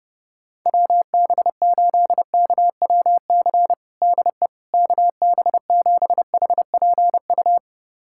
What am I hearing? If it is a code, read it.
Morse code, heard as a keyed tone, about 32 wpm, 700 Hz. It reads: WB8KWC DE KB7HPU
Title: USA Callsign DE Callsign - 6 Characters - 30wpm